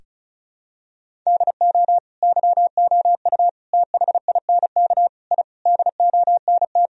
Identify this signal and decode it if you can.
Morse code, heard as a keyed tone, about 36 wpm, 700 Hz. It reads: DO YOU THINK I DONT
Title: Sentences from Top 1000 Words - Review Part 1 - 35wpm